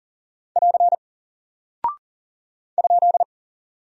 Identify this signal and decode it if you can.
Morse code, heard as a keyed tone, about 40 wpm, 700 Hz. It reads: <AR> ?
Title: Single Character (Letter:Number:Punctuation:Prosign) Alt 40wpm